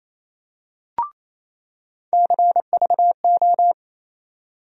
Morse code, heard as a keyed tone, about 28 wpm, 700 Hz. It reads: CVO
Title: USA Call Sign Postfixes - 3 Letters 28wpm